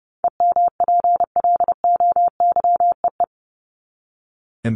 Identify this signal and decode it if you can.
Morse code, heard as a keyed tone, about 30 wpm, 700 Hz. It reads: EMPLOYEE
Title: Top 1501-1750 Words 30wpm